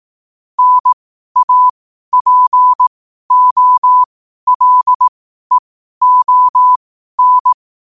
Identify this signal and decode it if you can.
Morse code, heard as a keyed tone, about 18 wpm, spaced as for 12 wpm, 1000 Hz. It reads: NAPOLEON
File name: qotdfiles / 12wpm